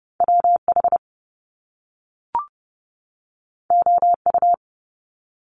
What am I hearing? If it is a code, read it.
Morse code, heard as a keyed tone, about 30 wpm, 700 Hz. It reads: WH OU
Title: Top Two Letter Combinations in Words 30wpm